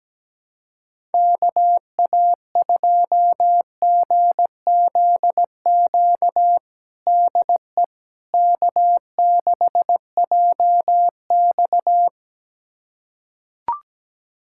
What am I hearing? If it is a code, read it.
Morse code, heard as a keyed tone, about 17 wpm, 700 Hz. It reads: KA2GZQ DE K6JX
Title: USA Callsign DE Callsign 17wpm